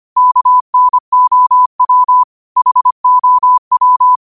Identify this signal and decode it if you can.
Morse code, heard as a keyed tone, about 25 wpm, 1000 Hz. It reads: KNOW HOW
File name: 25wpm